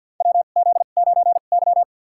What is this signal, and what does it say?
Morse code, heard as a keyed tone, about 51 wpm, spaced as for 36 wpm, 700 Hz. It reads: WZ9Y
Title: USA Call Signs - 4 Characters - 2x Character Spacing - 50wpm